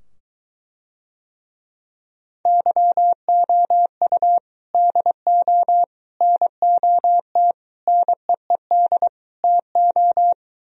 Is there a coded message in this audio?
Yes: Morse code from 2.5 to 10.3 s, heard as a keyed tone, about 23 wpm, 700 Hz. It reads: YOU DO NOT NEED TO